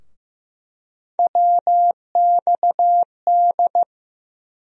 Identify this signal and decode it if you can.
Morse code, heard as a keyed tone, about 15 wpm, 700 Hz. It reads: WXD
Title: USA Call Sign Postfixes - 3 Letters 15wpm